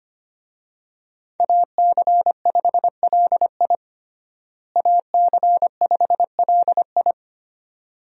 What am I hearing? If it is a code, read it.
Morse code, heard as a keyed tone, about 25 wpm, 700 Hz. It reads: AC5LI AC5LI
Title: USA Call Signs - 5 Characters - Repeated 2x - 25wpm